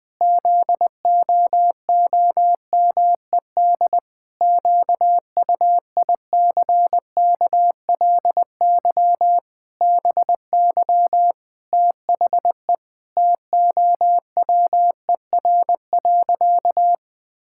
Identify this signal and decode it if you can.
Morse code, heard as a keyed tone, about 21 wpm, 700 Hz. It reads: ZOOMED QUICKLY BY THE TOWER.
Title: Pangram Challenge v2 - 20wpm